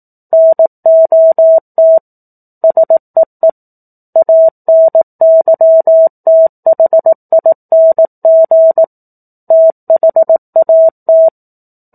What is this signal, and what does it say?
Morse code, heard as a keyed tone, about 18 wpm, 650 Hz. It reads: NOT SEE ANYTHING THAT
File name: 04-Chapter-04 18 WPM